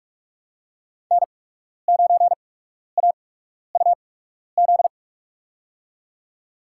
Morse code, heard as a keyed tone, about 42 wpm, spaced as for 12 wpm, 700 Hz. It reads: N9AUZ
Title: USA Call Signs - 5 Characters - 8x Character Spacing - 45wpm